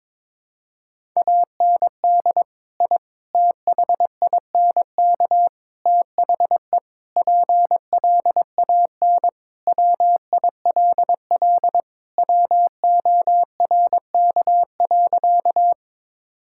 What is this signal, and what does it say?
Morse code, heard as a keyed tone, about 22 wpm, 700 Hz. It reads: AND I THINK THE PLAN WILL WORK.
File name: practice / 22wpm / Sentences with 4-Word Phrases - 22wpm